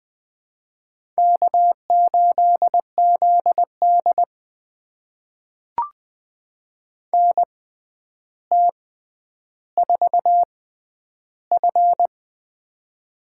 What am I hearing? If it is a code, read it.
Morse code, heard as a keyed tone, about 20 wpm, spaced as for 10 wpm, 700 Hz. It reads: K8ZD NT4F
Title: USA Call Signs - 4 Characters - 6x Character Spacing - 20wpm